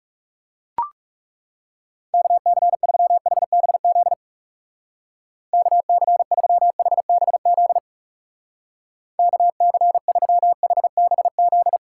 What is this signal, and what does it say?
Morse code, heard as a keyed tone, about 37 wpm, 700 Hz. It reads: KC3HBZ KC3HBZ KC3HBZ
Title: Speed Racing - USA Call Signs 30wpm